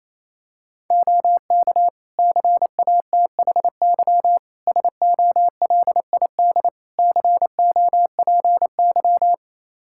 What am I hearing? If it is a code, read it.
Morse code, heard as a keyed tone, about 28 wpm, 700 Hz. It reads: OK CATHY SOLID COPY
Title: K1USN SST QSO Practice 28wpm